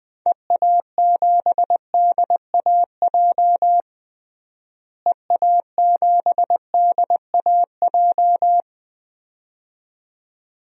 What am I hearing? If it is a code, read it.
Morse code, heard as a keyed tone, about 20 wpm, 700 Hz. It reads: EA7DAJ EA7DAJ
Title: DXCC-100 Call Signs - 6 Characters - Repeated 2x - 20wpm